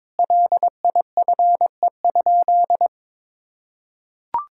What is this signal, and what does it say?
Morse code, heard as a keyed tone, about 22 wpm, 700 Hz. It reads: LIFE?
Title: Sentences from Top 1000 Words - Review Part 5 - 22wpm